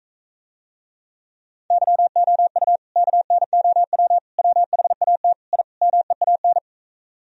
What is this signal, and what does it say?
Morse code, heard as a keyed tone, about 44 wpm, 700 Hz. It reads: YOU KNOW WHAT I MEAN